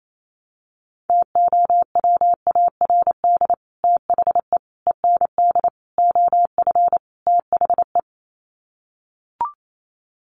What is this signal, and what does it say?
Morse code, heard as a keyed tone, about 28 wpm, 700 Hz. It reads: TOWARD THE END OF THE
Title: Sets of 5 Words 28wpm